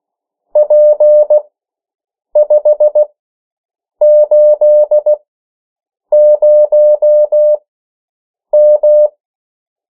Morse code, heard as a keyed tone, about 16 wpm, spaced as for 7 wpm, 600 Hz. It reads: P580M
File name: morse mp3 / 14 urok 13